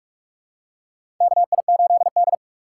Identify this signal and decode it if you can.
Morse code, heard as a keyed tone, about 46 wpm, 700 Hz. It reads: KI8D
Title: USA Call Signs - 4 Characters - 4x Character Spacing - 45wpm